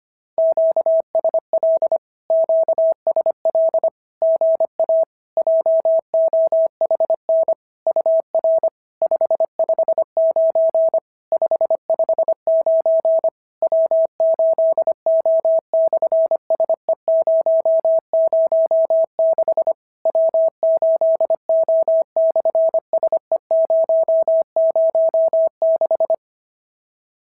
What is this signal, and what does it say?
Morse code, heard as a keyed tone, about 25 wpm, 650 Hz. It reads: QSL QSL GA JOHN UR 559 559 W8O/SE006 W8O/SE006
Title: SOTA QSO Practice 25wpm